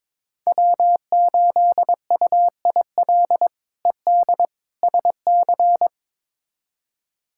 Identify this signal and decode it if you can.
Morse code, heard as a keyed tone, about 22 wpm, 700 Hz. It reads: W8UIL ED SC
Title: K1USN SST QSO Practice - 22wpm